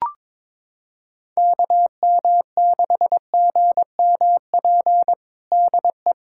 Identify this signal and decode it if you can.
Morse code, heard as a keyed tone, about 22 wpm, 700 Hz. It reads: KM6GMP DE
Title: USA Callsign DE Callsign - 6 Characters - 22wpm